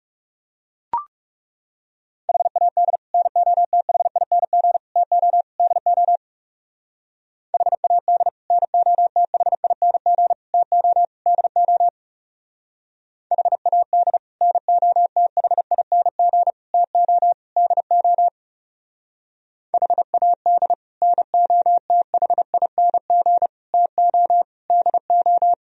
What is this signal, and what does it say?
Morse code, heard as a keyed tone, about 38 wpm, 700 Hz. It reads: HAD NOTHING TO DO HAD NOTHING TO DO HAD NOTHING TO DO HAD NOTHING TO DO